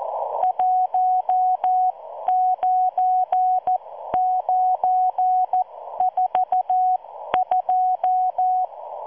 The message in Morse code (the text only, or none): 19942